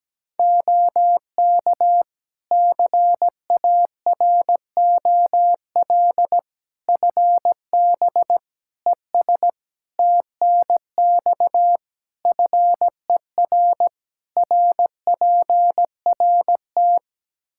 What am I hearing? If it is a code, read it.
Morse code, heard as a keyed tone, about 17 wpm, 700 Hz. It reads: OK CAROL FB ES TNX FER RPRT
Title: K1USN SST QSO Practice 17wpm